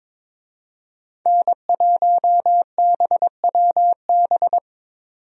Morse code, heard as a keyed tone, about 22 wpm, 700 Hz. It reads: N1BWB
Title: USA Call Signs - 5 Characters - 22wpm